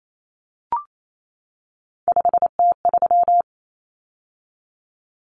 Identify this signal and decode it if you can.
Morse code, heard as a keyed tone, about 28 wpm, 700 Hz. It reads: 5T3
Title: Common Cut Numbers - 3-Digit 28wpm